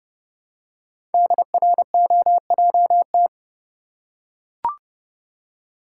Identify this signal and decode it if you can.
Morse code, heard as a keyed tone, about 30 wpm, 690 Hz. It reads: DROJT
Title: Code Groups - 5 Letters - 30wpm